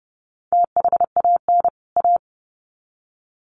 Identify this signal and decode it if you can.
Morse code, heard as a keyed tone, about 29 wpm, 700 Hz. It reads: THAN A
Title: Sets of 2 Words 30wpm